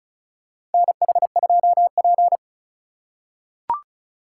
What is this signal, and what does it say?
Morse code, heard as a keyed tone, about 36 wpm, 700 Hz. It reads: NH2P
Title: USA Call Signs - 4 Characters - 2x Character Spacing - 35wpm